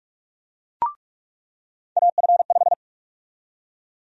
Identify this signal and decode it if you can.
Morse code, heard as a keyed tone, about 46 wpm, 700 Hz. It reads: AF5
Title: USA Call Sign Prefixes - 45wpm